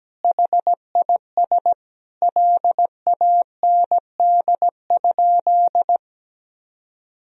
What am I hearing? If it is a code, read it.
Morse code, heard as a keyed tone, about 17 wpm, 700 Hz. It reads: HIS LAND?
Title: Sentences from Top 1000 Words - Review Part 2 - 17wpm